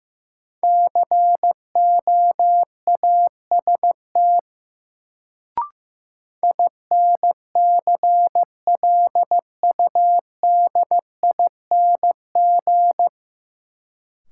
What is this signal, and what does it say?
Morse code, heard as a keyed tone, about 15 wpm, 700 Hz. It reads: COAST INCLUDING
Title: Sentences from Top 1000 Words - Review Part 4 - New Word Drill - 15wpm